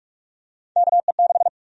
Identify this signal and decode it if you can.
Morse code, heard as a keyed tone, about 46 wpm, 700 Hz. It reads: KE6
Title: USA Call Sign Prefixes - 45wpm